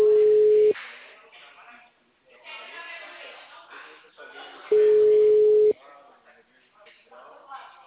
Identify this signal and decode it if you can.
Ringback tone (Europe, 425 Hz), with speech.
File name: out- -2002- - - .92337